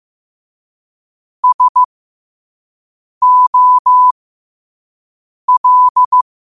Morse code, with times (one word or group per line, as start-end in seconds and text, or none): 1.4-1.8 S
3.2-4.1 O
5.5-6.2 L